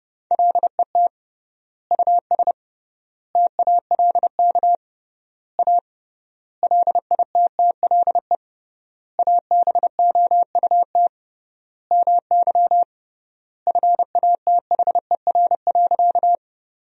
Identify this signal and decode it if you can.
Morse code, heard as a keyed tone, about 30 wpm, 700 Hz. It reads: LET US TALK A LITTLE ABOUT MY FATHER.
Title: 3x Word-Spacing - Sentences from Top 400 Words - 30wpm